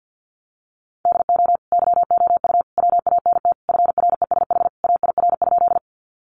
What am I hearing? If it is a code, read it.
Morse code, heard as a keyed tone, about 50 wpm, 700 Hz. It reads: DO YOU WANT FRESH AIR?
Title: Sentences from Top 1200 Words 50wpm